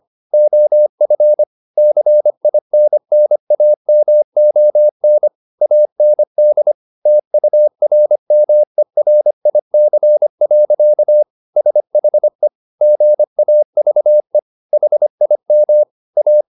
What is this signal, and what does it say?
Morse code, heard as a keyed tone, about 25 wpm, 600 Hz. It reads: OF CINNAMON AND TURMERIC. SHE GAVE HIM A